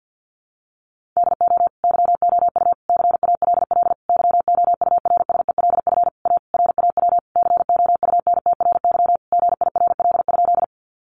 Morse code, heard as a keyed tone, about 50 wpm, 700 Hz. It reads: DO YOU CALL YOURSELF A RAW COUNTRY GIRL?
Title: Sentences from Top 1500 Words 50wpm